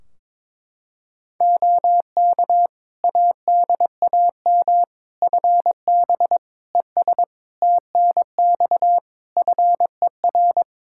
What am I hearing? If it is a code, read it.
Morse code, heard as a keyed tone, about 22 wpm, 700 Hz. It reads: OK ADAM FB ES TNX FER
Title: Practice QSO - Exchange 2 - Info - 22wpm